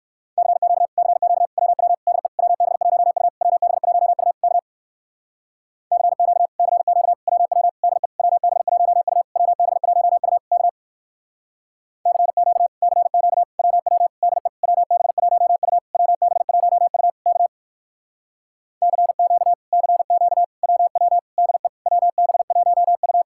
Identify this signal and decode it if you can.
Morse code, heard as a keyed tone, about 58 wpm, 700 Hz. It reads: CQ CQ WW DE WB1U WB1U K CQ CQ WW DE WB1U WB1U K CQ CQ WW DE WB1U WB1U K CQ CQ WW DE WB1U